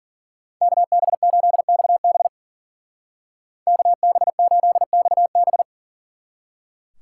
Morse code, heard as a keyed tone, about 50 wpm, 700 Hz. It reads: KB8XB KB8XB
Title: Speed Racing - USA Call Signs 40wpm